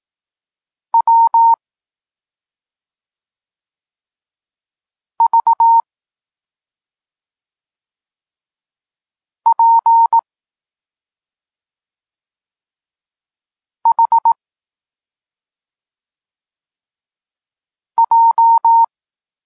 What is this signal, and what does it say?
Morse code, heard as a keyed tone, about 18 wpm, 920 Hz. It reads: W V P H J
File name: Tekst-34 4